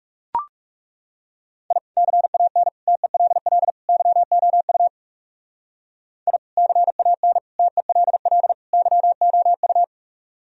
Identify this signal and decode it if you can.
Morse code, heard as a keyed tone, about 37 wpm, 700 Hz. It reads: I CAN TELL YOU I CAN TELL YOU
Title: Speed-Racing - Sets of 4 Words 30wpm